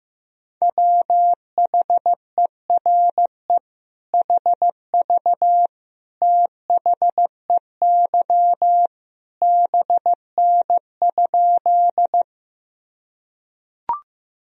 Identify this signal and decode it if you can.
Morse code, heard as a keyed tone, about 15 wpm, 700 Hz. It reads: WHERE HV THEY BN?
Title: Sentences from Top 200 Words and Abbreviations 15wpm